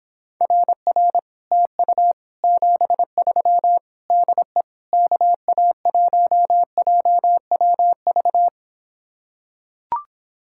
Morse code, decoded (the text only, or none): RR TU 73 DE KA1JWV